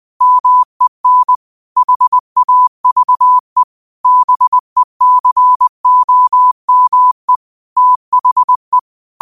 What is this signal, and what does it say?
Morse code, heard as a keyed tone, about 20 wpm, 1000 Hz. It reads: MEN HAVE BECOME THE